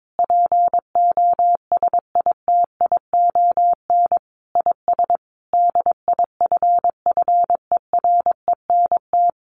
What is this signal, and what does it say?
Morse code, heard as a keyed tone, about 22 wpm, 700 Hz. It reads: POSITION IS DIFFERENT
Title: Sentences from Top 1400 Words 22wpm